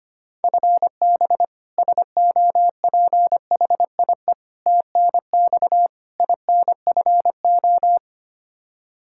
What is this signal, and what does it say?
Morse code, heard as a keyed tone, about 25 wpm, 700 Hz. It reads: FB SOPHIE TNX INFO
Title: K1USN SST QSO Practice 25wpm